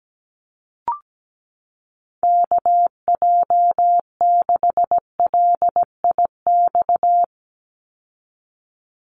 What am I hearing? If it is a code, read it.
Morse code, heard as a keyed tone, about 17 wpm, 700 Hz. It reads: KJ6LIX
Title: USA Call Signs 17wpm